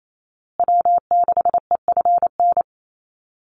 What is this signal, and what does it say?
Morse code, heard as a keyed tone, about 28 wpm, 700 Hz. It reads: W6EFN
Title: USA Call Signs Repeated 3x 28wpm